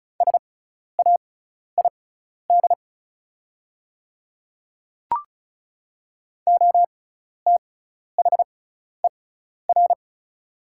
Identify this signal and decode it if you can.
Morse code, heard as a keyed tone, about 36 wpm, spaced as for 12 wpm, 700 Hz. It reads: SAID OTHER
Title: 6x Character Spacing - Top 100 Words - 35wpm